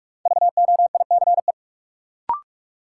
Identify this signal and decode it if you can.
Morse code, heard as a keyed tone, about 46 wpm, 700 Hz. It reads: VOICE
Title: Sentences from Top 500 Words - New Word Drill - 45wpm